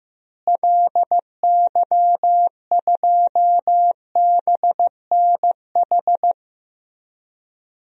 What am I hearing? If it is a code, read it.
Morse code, heard as a keyed tone, about 15 wpm, 700 Hz. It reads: LY2BNH
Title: DXCC-100 Call Signs - 6 Characters - Repeated 3x - 15wpm